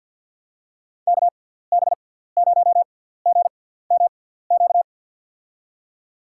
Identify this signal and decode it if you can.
Morse code, heard as a keyed tone, about 53 wpm, spaced as for 17 wpm, 700 Hz. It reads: KB0GMQ
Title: USA Call Signs - 6 Characters - 6x Character Spacing - 50wpm